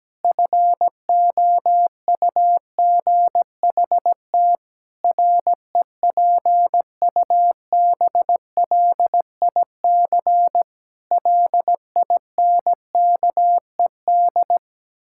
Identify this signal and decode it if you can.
Morse code, heard as a keyed tone, about 17 wpm, 700 Hz. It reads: FOUGHT REPUBLIC LINKED
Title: Pangram Challenge v2 - 17wpm